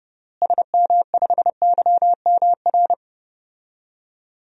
Morse code, heard as a keyed tone, about 32 wpm, 700 Hz. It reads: SM5YMR
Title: DXCC-100 Call Signs - 6 Characters - Repeated 3x - 30wpm